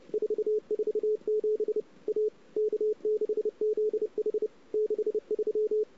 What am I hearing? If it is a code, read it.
Morse code, heard as a keyed tone, about 30 wpm, 410 Hz. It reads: 447 A K6ZH 63
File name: CW / 2008-ss-cw-k5tr-op-n5rz -05:00